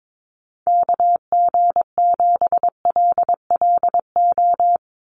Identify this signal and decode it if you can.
Morse code, heard as a keyed tone, about 22 wpm, 700 Hz. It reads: KG7LLO
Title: USA Call Signs 22wpm